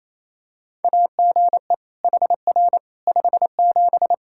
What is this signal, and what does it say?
Morse code, heard as a keyed tone, about 27 wpm, 700 Hz. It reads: AGE HR 57
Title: K1USN SST QSO Practice 28wpm